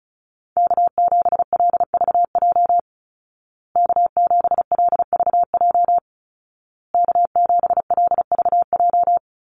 Morse code, heard as a keyed tone, about 34 wpm, 700 Hz. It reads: K7LVJ K7LVJ K7LVJ
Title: Call Signs Repeated 2x - 35wpm